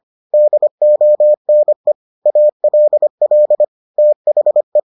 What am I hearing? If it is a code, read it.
Morse code, heard as a keyed tone, about 25 wpm, 600 Hz. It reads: DONE ALL THE